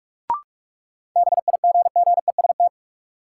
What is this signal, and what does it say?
Morse code, heard as a keyed tone, about 46 wpm, 700 Hz. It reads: BIGGEST
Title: Sets of 4 Words - New Word Drill - 45wpm